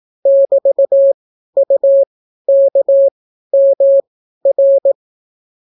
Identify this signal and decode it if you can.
Morse code, heard as a keyed tone, about 18 wpm, spaced as for 12 wpm, 550 Hz. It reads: <BT>UKMR